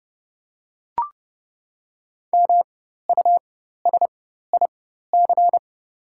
Morse code, heard as a keyed tone, about 30 wpm, spaced as for 14 wpm, 700 Hz. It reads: MUSIC
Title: 4x Character Spacing - Top 500 Words - 30wpm